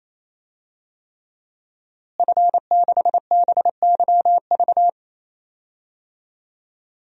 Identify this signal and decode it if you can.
Morse code, heard as a keyed tone, about 28 wpm, 700 Hz. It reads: F6BYV
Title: DXCC-100 Call Signs - 5 Characters - Repeated 3x - 28wpm